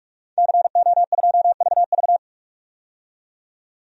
Morse code, heard as a keyed tone, about 47 wpm, 700 Hz. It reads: CO2VV
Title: DXCC-100 Call Signs - 5 Characters - 45wpm